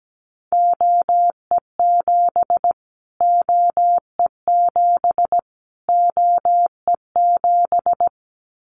Morse code, heard as a keyed tone, about 17 wpm, 700 Hz. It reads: OE7 OE7 OE7
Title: Callsign Prefixes for Top 25 DXCC - Repeated 3x - 17wpm